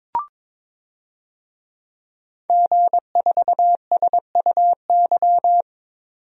Morse code, heard as a keyed tone, about 22 wpm, 700 Hz. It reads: G4SUY
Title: DXCC-100 Call Signs - 5 Characters - 22wpm